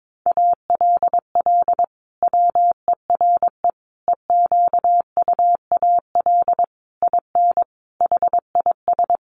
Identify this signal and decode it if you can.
Morse code, heard as a keyed tone, about 22 wpm, 700 Hz. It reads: ALL WERE EQUAL IN HIS
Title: Sentences from Top 1400 Words 22wpm